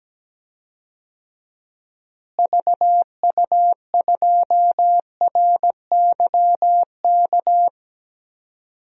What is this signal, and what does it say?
Morse code, heard as a keyed tone, about 17 wpm, 700 Hz. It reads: VU2RYK
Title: DXCC-100 Call Signs - 6 Characters - Repeated 3x - 17wpm